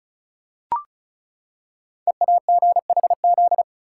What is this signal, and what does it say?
Morse code, heard as a keyed tone, about 36 wpm, 700 Hz. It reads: EAGHZ
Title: Code Groups - 5 Letters - 35wpm